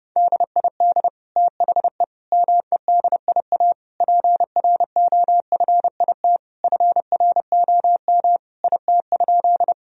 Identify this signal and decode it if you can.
Morse code, heard as a keyed tone, about 30 wpm, 700 Hz. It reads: DID THE MEDIA PROFIT FROM IT?
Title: Sentences from Top 1000 Words - Review Part 4 - 30wpm